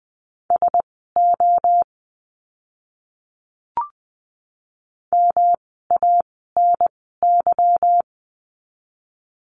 Morse code, heard as a keyed tone, about 20 wpm, spaced as for 15 wpm, 700 Hz. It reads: SO MANY
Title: 230 Lesson 37f - Letter x - Words 20wpm